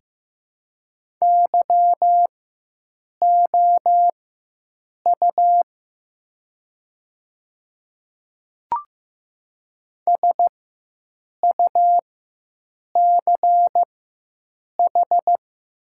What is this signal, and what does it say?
Morse code, heard as a keyed tone, about 15 wpm, spaced as for 7 wpm, 700 Hz. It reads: YOU SUCH
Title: 4x Character Spacing - Top 300 Words - 15wpm